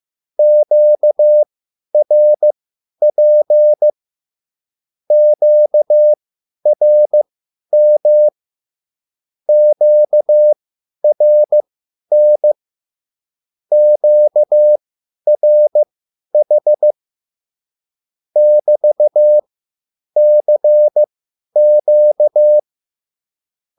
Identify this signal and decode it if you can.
Morse code, heard as a keyed tone, about 15 wpm, spaced as for 10 wpm, 600 Hz. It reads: QRP QRM QRN QRH <BT>CQ